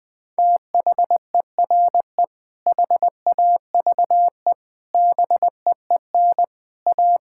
Morse code, heard as a keyed tone, about 20 wpm, 700 Hz. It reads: THERE HAVE BEEN A
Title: Sentences with 6-Word Phrases - 20wpm